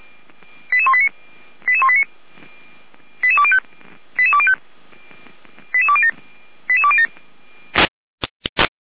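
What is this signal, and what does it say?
Five-tone selective call (ZVEI), heard as quick runs of five tones, tones of about 70 ms. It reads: 88189 88189 88205 88205 88207 88207
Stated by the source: ZVEI five-tone selective call received via SDR, archived by the Feuerwehr Orlamünde fire brigade.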